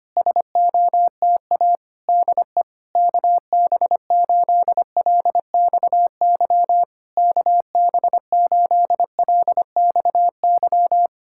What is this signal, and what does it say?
Morse code, heard as a keyed tone, about 24 wpm, 700 Hz. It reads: SOTA DE KB8LXY KB8LXY
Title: Speed-Racing Common QSO Phrases - 25wpm